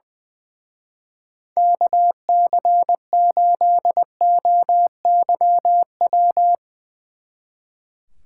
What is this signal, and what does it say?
Morse code, heard as a keyed tone, about 21 wpm, 700 Hz. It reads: KC8OYW
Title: Speed Racing - USA 2x3 Call Signs 20wpm